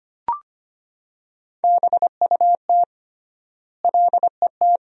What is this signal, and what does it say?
Morse code, heard as a keyed tone, about 25 wpm, 700 Hz. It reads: BUT LET
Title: 3x Word-Spacing - Sentences from Top 400 Words - 25wpm